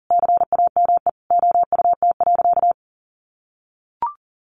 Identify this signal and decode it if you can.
Morse code, heard as a keyed tone, about 40 wpm, 700 Hz. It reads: CAME OUT.
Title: Sentences from Top 1300 Words 40wpm